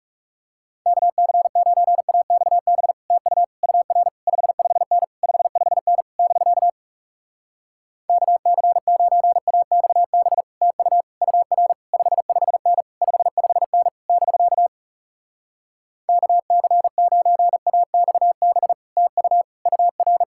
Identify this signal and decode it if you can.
Morse code, heard as a keyed tone, about 41 wpm, 700 Hz. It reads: KC9AXB TU UR 55N 55N <BK> KC9AXB TU UR 55N 55N <BK> KC9AXB TU UR